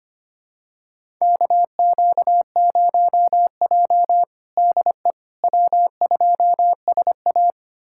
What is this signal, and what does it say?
Morse code, heard as a keyed tone, about 25 wpm, 700 Hz. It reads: KQ0J DE W2SA
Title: USA Callsign DE Callsign - 4 Characters - 25wpm